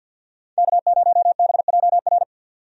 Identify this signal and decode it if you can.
Morse code, heard as a keyed tone, about 46 wpm, 700 Hz. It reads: K0BJR
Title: USA Call Signs - 5 Characters - 50wpm